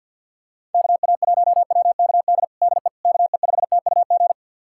Morse code, heard as a keyed tone, about 49 wpm, 690 Hz. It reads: KA1WKD DE KE5TUG
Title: USA Callsign DE Callsign - 6 Characters - 50wpm